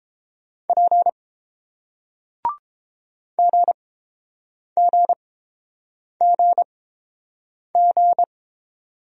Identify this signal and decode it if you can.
Morse code, heard as a keyed tone, about 30 wpm, spaced as for 7 wpm, 690 Hz. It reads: P GGGG